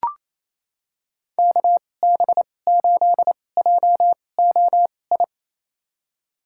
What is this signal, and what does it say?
Morse code, heard as a keyed tone, about 28 wpm, spaced as for 20 wpm, 700 Hz. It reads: KB8JOI
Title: USA Call Signs - 6 Characters - 2x Character Spacing - 28wpm